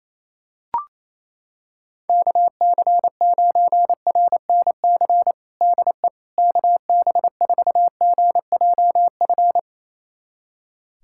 Morse code, heard as a keyed tone, about 28 wpm, 700 Hz. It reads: KC9RNC DE KB4GJF
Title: USA Callsign DE Callsign 28wpm